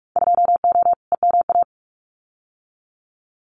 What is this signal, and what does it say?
Morse code, heard as a keyed tone, about 44 wpm, 700 Hz. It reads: 2O EMA
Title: Winter Field Day Exchanges 45wpm